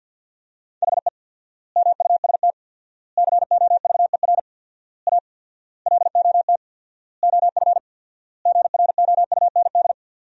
Morse code, heard as a keyed tone, about 53 wpm, 700 Hz. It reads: HE MUST COVER A LOT OF GROUND
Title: 4x Word-Spacing - Sentences from Top 500 Words - 50wpm